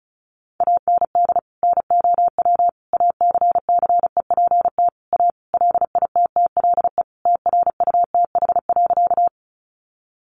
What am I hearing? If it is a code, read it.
Morse code, heard as a keyed tone, about 34 wpm, 700 Hz. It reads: AND NOW ACCEPT A LITTLE TRUTH.
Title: Sentences from Top 1300 Words 35wpm